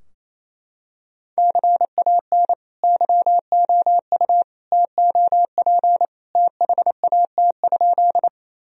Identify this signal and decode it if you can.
Morse code, heard as a keyed tone, about 28 wpm, 700 Hz. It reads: CAN YOU TOP THAT?
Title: Sentences from Top 1000 Words - Review Part 2 - 28wpm